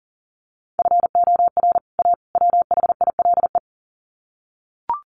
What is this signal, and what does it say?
Morse code, heard as a keyed tone, about 40 wpm, 700 Hz. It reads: FOR A WHILE
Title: Sets of 3 Words 40wpm